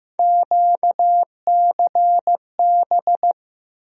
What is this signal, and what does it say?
Morse code, heard as a keyed tone, about 15 wpm, 700 Hz. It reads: QCB
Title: USA Call Sign Postfixes - 3 Letters 15wpm